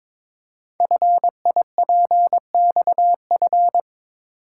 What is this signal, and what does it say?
Morse code, heard as a keyed tone, about 22 wpm, 700 Hz. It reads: FIPXF
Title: Code Groups - 5 Letters - 22wpm